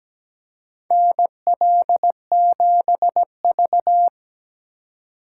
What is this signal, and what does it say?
Morse code, heard as a keyed tone, about 17 wpm, 700 Hz. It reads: NL7V
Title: K1USN SST QSO Practice - 17wpm